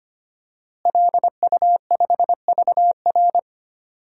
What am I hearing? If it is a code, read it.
Morse code, heard as a keyed tone, about 25 wpm, 700 Hz. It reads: LU5VR
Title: DXCC-100 Call Signs - 5 Characters - Repeated 3x - 25wpm